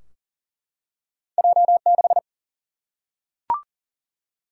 Morse code, heard as a keyed tone, about 40 wpm, 700 Hz. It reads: J6
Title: Single-Letter Single-Number 40wpm